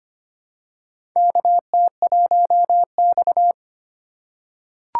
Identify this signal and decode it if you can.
Morse code, heard as a keyed tone, about 25 wpm, 700 Hz. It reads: KT1X
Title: USA Call Signs - 4 Characters - 3x Character Spacing - 25wpm